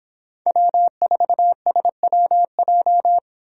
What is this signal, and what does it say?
Morse code, heard as a keyed tone, about 25 wpm, 700 Hz. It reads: W4SWJ